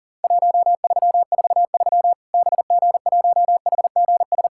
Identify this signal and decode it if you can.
Morse code, heard as a keyed tone, about 40 wpm, 700 Hz. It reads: 1343 BG1HGS
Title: Worked All Europe DX Contest - QTC Exchange - 40wpm